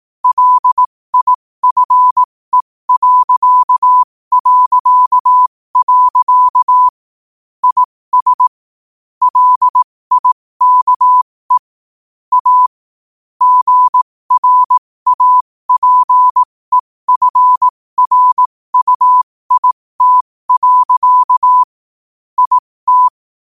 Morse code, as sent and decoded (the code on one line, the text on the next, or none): .-.. .. ..-. .-.-.- .-.-.- .-.-.- / .. ... / .-.. .. -.- . / .- / --. .-. .- .--. ..-. .-. ..- .. - .-.-.- / .. -
LIFE... IS LIKE A GRAPEFRUIT. IT